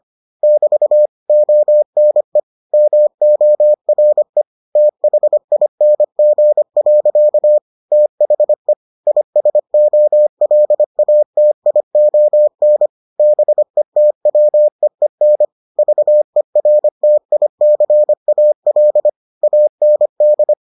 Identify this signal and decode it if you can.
Morse code, heard as a keyed tone, about 25 wpm, 600 Hz. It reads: <BT> ONE MORE THING. THE ISOLATION BETWEEN VERTICAL AND